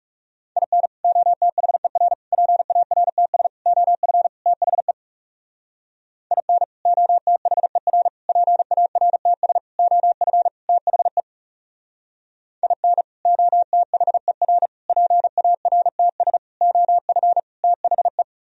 Morse code, heard as a keyed tone, about 36 wpm, 700 Hz. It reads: IN OTHER PARTS OF THE IN OTHER PARTS OF THE IN OTHER PARTS OF THE
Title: Speed-Racing - Sets of 5 Words 30wpm